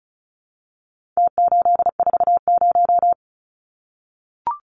Morse code, heard as a keyed tone, about 34 wpm, 700 Hz. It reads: T840
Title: Numbers - Contest Style - 4-Digit 35wpm